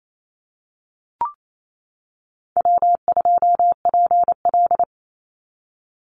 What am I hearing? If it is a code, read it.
Morse code, heard as a keyed tone, about 28 wpm, 700 Hz. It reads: W2PL
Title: Call Signs - 28wpm